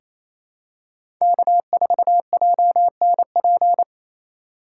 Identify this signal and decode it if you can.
Morse code, heard as a keyed tone, about 28 wpm, 700 Hz. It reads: K4JNP
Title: USA Call Signs - 5 Characters - 4x Character Spacing - 28wpm